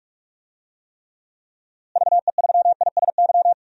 Morse code, heard as a keyed tone, about 45 wpm, 700 Hz. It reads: VE3ISY
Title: DXCC-100 Call Signs - 6 Characters - Repeated 2x - 45wpm